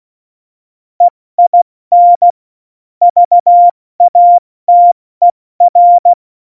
Morse code, heard as a keyed tone, about 16 wpm, 700 Hz. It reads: EIN VATER
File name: Morsekurs / Witze / Witz 11. Juni 2019, 70 F80